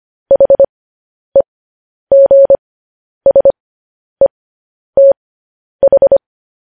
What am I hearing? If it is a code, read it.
Morse code, heard as a keyed tone, about 25 wpm, spaced as for 10 wpm, 550 Hz. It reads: HEGSETH